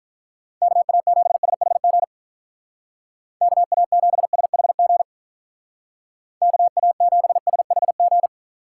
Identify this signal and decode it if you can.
Morse code, heard as a keyed tone, about 44 wpm, 700 Hz. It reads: KA7SHG KA7SHG KA7SHG